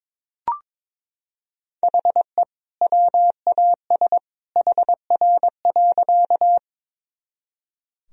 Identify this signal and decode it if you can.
Morse code, heard as a keyed tone, about 22 wpm, 700 Hz. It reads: HE WAS HR.
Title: Sentences from Top 200 Words and Abbreviations 22wpm